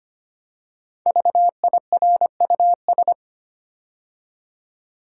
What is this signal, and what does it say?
Morse code, heard as a keyed tone, about 25 wpm, 700 Hz. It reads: VIRUS
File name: 5 Letter Words - First 3 Letters Given - 25wpm